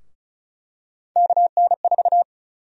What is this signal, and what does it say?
Morse code, heard as a keyed tone, about 34 wpm, 700 Hz. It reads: KN4
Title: USA Call Sign Prefixes - 35wpm